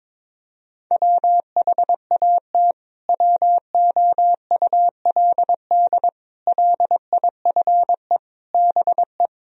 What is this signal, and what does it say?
Morse code, heard as a keyed tone, about 21 wpm, 700 Hz. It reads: WHAT WOULD LIFE BE
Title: Sentences from Top 1000 Words - Review Part 3 - 22wpm